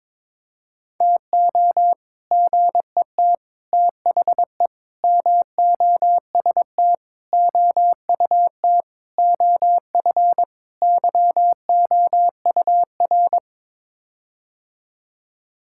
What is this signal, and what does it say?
Morse code, heard as a keyed tone, about 21 wpm, 700 Hz. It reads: TO GET THE MOST OUT OF YOUR
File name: practice / 22wpm / Sets of 7 Words - 22wpm